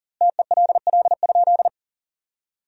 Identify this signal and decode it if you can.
Morse code, heard as a keyed tone, about 40 wpm, 700 Hz. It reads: TELL?
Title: 2x Word-Spacing - Sentences from Top 300 Words - 40wpm